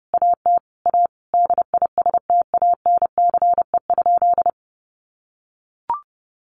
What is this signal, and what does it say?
Morse code, heard as a keyed tone, about 30 wpm, 700 Hz. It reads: AT A DISTANCE?
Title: Sentences from Top 1200 Words 30wpm